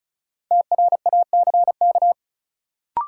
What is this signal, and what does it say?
Morse code, heard as a keyed tone, about 34 wpm, 700 Hz. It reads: TRACK
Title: Sentences from Top 1100 Words - New Word Drill Alt - 35wpm